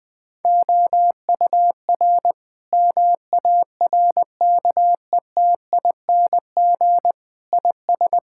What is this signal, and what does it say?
Morse code, heard as a keyed tone, about 20 wpm, 700 Hz. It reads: OUR MARKETING IS
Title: Sentences from Top 1000 Words - Review Part 5 - 20wpm